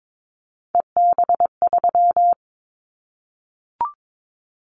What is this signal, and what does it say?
Morse code, heard as a keyed tone, about 22 wpm, 700 Hz. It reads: EB3
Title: Callsign Prefixes for Top 25 DXCC - 22wpm